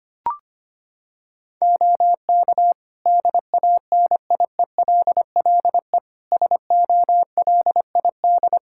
Morse code, heard as a keyed tone, about 25 wpm, 700 Hz. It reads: OK DANIELLE SOLID
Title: Practice QSO - Exchange 3 - More Info - 25wpm